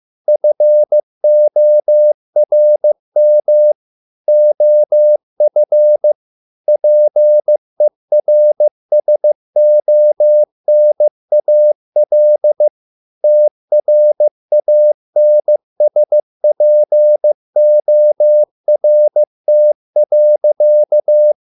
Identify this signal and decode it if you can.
Morse code, heard as a keyed tone, about 15 wpm, 600 Hz. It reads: FORM OF PERSONAL TRANSPORT.